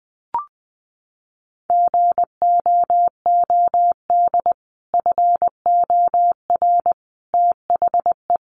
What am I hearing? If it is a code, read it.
Morse code, heard as a keyed tone, about 20 wpm, 700 Hz. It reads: GOOD FOR THE
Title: Sentences from Top 300 Words 20wpm